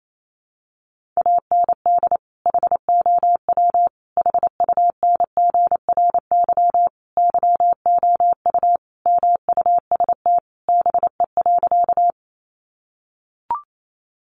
Morse code, heard as a keyed tone, about 28 wpm, 700 Hz. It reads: AND HOW HUNGRY YOU MUST BE.
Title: Sentences from Top 1400 Words 28wpm